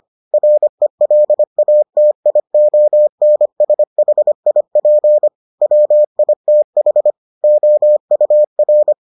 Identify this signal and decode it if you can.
Morse code, heard as a keyed tone, about 25 wpm, 600 Hz. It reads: RELATIONSHIP WITH OUR